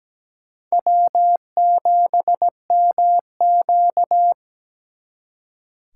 Morse code, heard as a keyed tone, about 20 wpm, 700 Hz. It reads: W7MQ
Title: Speed Racing - USA 1x2 Call Signs 17wpm